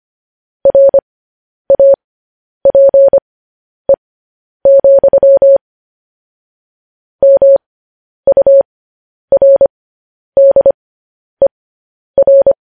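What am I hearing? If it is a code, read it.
Morse code, heard as a keyed tone, about 25 wpm, spaced as for 10 wpm, 550 Hz. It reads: RAPE, MURDER
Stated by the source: Morse code news headlines sent at 10.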